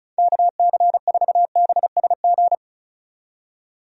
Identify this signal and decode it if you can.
Morse code, heard as a keyed tone, about 36 wpm, 700 Hz. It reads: KC4BSG
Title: USA Call Signs - 6 Characters - 35wpm